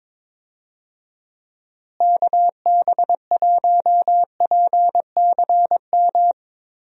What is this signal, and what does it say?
Morse code, heard as a keyed tone, about 21 wpm, 700 Hz. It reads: KB1PCM
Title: USA Call Signs - 6 Characters - Repeated 3x - 22wpm